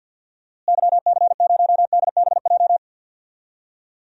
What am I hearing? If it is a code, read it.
Morse code, heard as a keyed tone, about 53 wpm, 700 Hz. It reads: YC0DBJ
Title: DXCC-100 Call Signs - 6 Characters - Repeated 2x - 50wpm